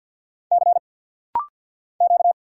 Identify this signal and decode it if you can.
Morse code, heard as a keyed tone, about 50 wpm, 690 Hz. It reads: C Q
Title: Single Letter-Number Alt - Rapid-Fire 50wpm